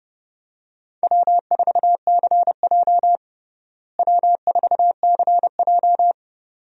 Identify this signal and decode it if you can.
Morse code, heard as a keyed tone, about 30 wpm, 700 Hz. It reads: W4CJ W4CJ
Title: USA Call Signs - 4 Characters - Repeated 2x - 30wpm